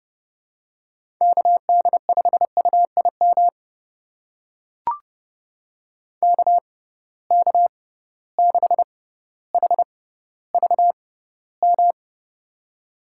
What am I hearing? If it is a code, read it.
Morse code, heard as a keyed tone, about 30 wpm, spaced as for 15 wpm, 700 Hz. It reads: KD5UIM KK6HVM
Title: USA Call Signs - 6x Character Spacing - 30wpm